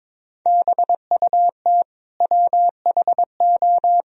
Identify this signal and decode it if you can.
Morse code, heard as a keyed tone, about 22 wpm, 700 Hz. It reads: BUT WHO
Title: Sentences from Top 1000 Words - Review Part 1 - 22wpm